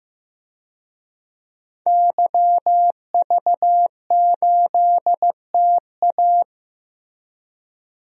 Morse code, heard as a keyed tone, about 15 wpm, 700 Hz. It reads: YV8TA
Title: DXCC-100 Call Signs - 5 Characters - 15wpm